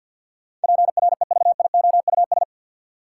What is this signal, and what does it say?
Morse code, heard as a keyed tone, about 48 wpm, 700 Hz. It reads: PREVIOUS
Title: Sentences from Top 1500 Words - New Word Drill - 50wpm